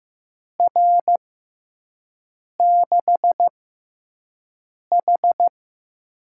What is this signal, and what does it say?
Morse code, heard as a keyed tone, about 15 wpm, spaced as for 5 wpm, 700 Hz. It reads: R6H
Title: USA Call Signs - 4 Characters - 6x Character Spacing - 15wpm